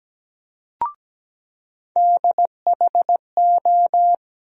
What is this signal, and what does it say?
Morse code, heard as a keyed tone, about 17 wpm, 700 Hz. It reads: DHO
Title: USA Call Sign Postfixes - 3 Letters 17wpm